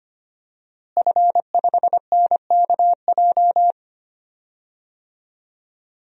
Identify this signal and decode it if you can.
Morse code, heard as a keyed tone, about 25 wpm, 700 Hz. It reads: F5NKJ
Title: DXCC-100 Call Signs - 5 Characters - Repeated 3x - 25wpm